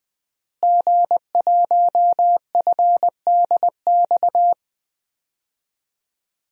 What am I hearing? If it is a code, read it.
Morse code, heard as a keyed tone, about 20 wpm, 700 Hz. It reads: G1FDX
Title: DXCC-100 Call Signs - 5 Characters - Repeated 3x - 20wpm